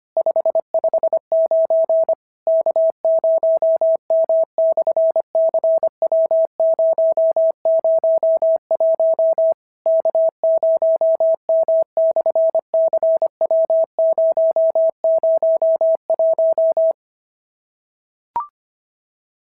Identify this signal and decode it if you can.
Morse code, heard as a keyed tone, about 25 wpm, 650 Hz. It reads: 559 K0M/CW001 K0M/CW001
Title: SOTA QSO Practice 25wpm